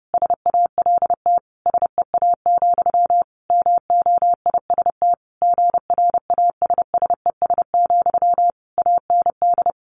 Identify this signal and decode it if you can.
Morse code, heard as a keyed tone, about 30 wpm, 700 Hz. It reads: SALT SEA, MOIST GRASSES, AND
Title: A Sound Of Thunder 30wpm